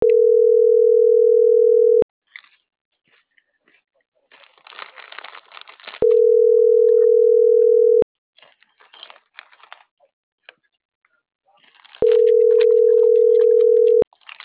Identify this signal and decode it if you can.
Ringback tone (North America), 440 + 480 Hz.